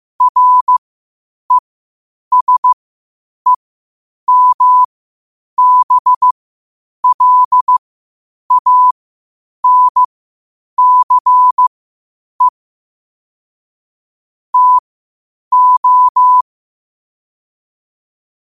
Morse code, heard as a keyed tone, about 15 wpm, spaced as for 8 wpm, 1000 Hz. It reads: RESEMBLANCE TO
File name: qotdfiles / 8wpm